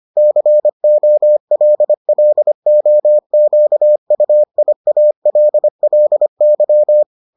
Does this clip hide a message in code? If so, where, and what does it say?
Yes: Morse code from 0.2 to 7.0 s, heard as a keyed tone, about 25 wpm, 600 Hz. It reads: COLLOQUIALLY